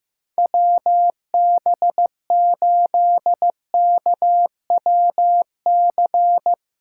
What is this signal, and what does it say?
Morse code, heard as a keyed tone, about 15 wpm, 700 Hz. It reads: WB8KWC
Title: USA Callsign DE Callsign - 6 Characters - 15wpm